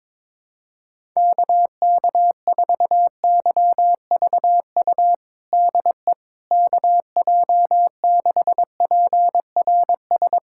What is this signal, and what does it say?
Morse code, heard as a keyed tone, about 22 wpm, 700 Hz. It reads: KK4YVU DE KJ6PRS
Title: USA Callsign DE Callsign 22wpm